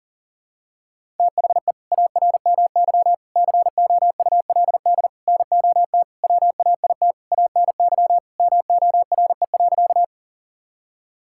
Morse code, heard as a keyed tone, about 37 wpm, 700 Hz. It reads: THE ARMY COULD NOT WAIT ANY MORE.
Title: Sentences from Top 1000 Words - Review Part 5 - 40wpm